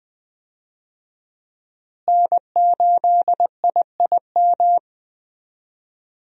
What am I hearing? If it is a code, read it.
Morse code, heard as a keyed tone, about 20 wpm, 700 Hz. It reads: N8IIM
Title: USA Call Signs - 5 Characters - Repeated 2x - 20wpm